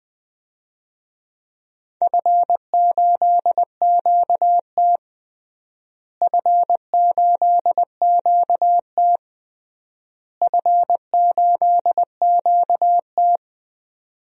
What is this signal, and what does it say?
Morse code, heard as a keyed tone, about 20 wpm, 700 Hz. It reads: F8QT F8QT F8QT
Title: DXCC-100 Call Signs - 4 Characters - Repeated 3x - 20wpm